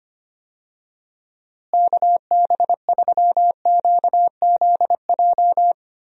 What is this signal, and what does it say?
Morse code, heard as a keyed tone, about 25 wpm, 700 Hz. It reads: KB3QZJ
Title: USA Call Signs - 6 Characters - Repeated 3x - 25wpm